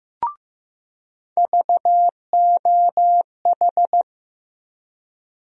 Morse code, heard as a keyed tone, about 15 wpm, 700 Hz. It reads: VOH
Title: USA Call Sign Postfixes - 3 Letters 15wpm